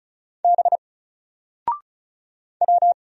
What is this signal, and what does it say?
Morse code, heard as a keyed tone, about 36 wpm, 700 Hz. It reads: B W
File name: Single-Letter-Number-35wpm